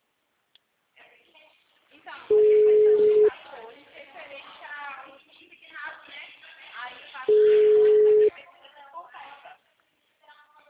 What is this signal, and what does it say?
Ringback tone (Europe, 425 Hz), with speech.
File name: out- -2020- - - .51719